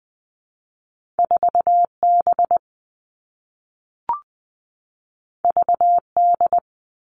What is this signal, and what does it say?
Morse code, heard as a keyed tone, about 20 wpm, 700 Hz. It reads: 4B VD
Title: All Cut Numbers - 2-Digit 20wpm